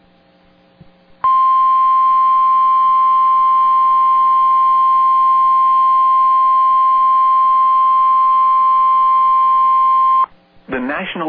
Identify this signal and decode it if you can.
Emergency Alert System (SAME) tones (the 1050 Hz attention tone), with speech. It is heard from 1.2 to 10.3 s.